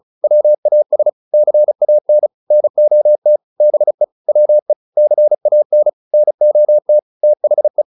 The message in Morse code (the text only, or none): WAS CAN NOT BE WE CAN NOT THE